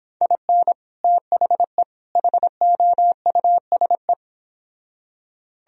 Morse code, heard as a keyed tone, about 26 wpm, 700 Hz. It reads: IN THE HOUSE